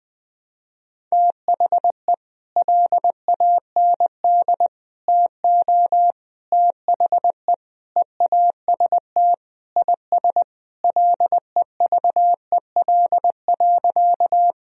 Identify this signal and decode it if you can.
Morse code, heard as a keyed tone, about 20 wpm, 700 Hz. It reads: THE LAND TO THE EAST IS LEVEL.
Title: Sentences from Top 1000 Words - Review Part 4 - 20wpm